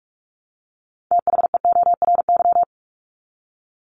Morse code, heard as a keyed tone, about 44 wpm, 700 Hz. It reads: THEORY
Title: Top 501-750 Words 45wpm